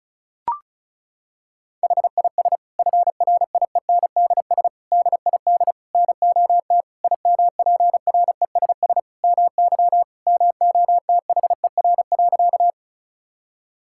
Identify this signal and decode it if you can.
Morse code, heard as a keyed tone, about 36 wpm, 700 Hz. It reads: HIS FRIENDS DID NOT IMPRESS MY MOTHER.
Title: Sentences from Top 1600 Words 35wpm